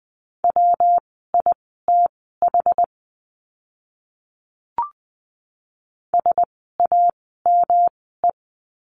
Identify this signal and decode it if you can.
Morse code, heard as a keyed tone, about 20 wpm, spaced as for 14 wpm, 700 Hz. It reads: WITH SAME
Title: 237 Lesson 38f - Letter q - Words 20wpm